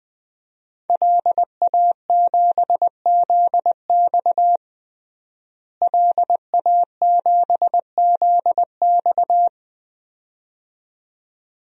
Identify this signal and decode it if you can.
Morse code, heard as a keyed tone, about 20 wpm, 700 Hz. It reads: LA7ZX LA7ZX
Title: DXCC-100 Call Signs - 5 Characters - Repeated 2x - 20wpm